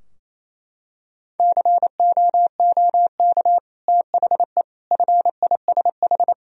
Morse code, heard as a keyed tone, about 28 wpm, 700 Hz. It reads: COOK THE FISH
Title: Sentences from Top 1000 Words - Review Part 4 - 28wpm